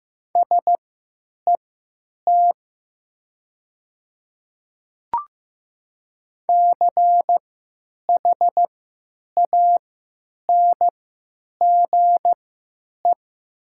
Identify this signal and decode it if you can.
Morse code, heard as a keyed tone, about 15 wpm, spaced as for 8 wpm, 700 Hz. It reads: SET CHANGE
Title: 3x Character Spacing - Top 200 Words - 15wpm